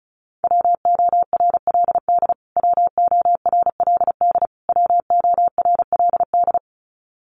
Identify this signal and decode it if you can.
Morse code, heard as a keyed tone, about 34 wpm, 700 Hz. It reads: WORLD WORLD WORLD
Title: Top 300 Words Repeated 35wpm